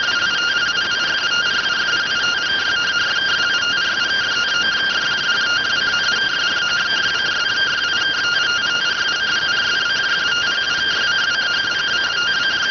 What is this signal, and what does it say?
Two-tone teleprinter signal, about 170 Hz shift.